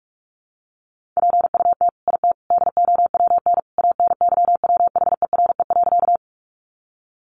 Morse code, heard as a keyed tone, about 44 wpm, 700 Hz. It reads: PUT IT DOWN ANYWHERE.
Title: Sentences from Top 1400 Words 45wpm